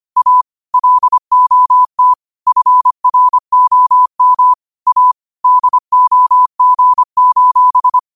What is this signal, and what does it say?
Morse code, heard as a keyed tone, about 25 wpm, 1000 Hz. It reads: A LOT FROM A DOG:
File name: qotdfiles / 25wpm